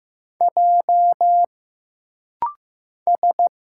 Morse code, heard as a keyed tone, about 15 wpm, 700 Hz. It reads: J S
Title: Single Letter-Number Alt - Rapid-Fire 15wpm